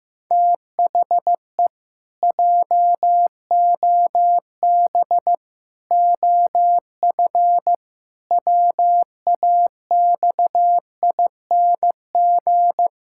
Morse code, heard as a keyed tone, about 15 wpm, 700 Hz. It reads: THE JOB OF WAXING
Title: Pangram Challenge v2 - 15wpm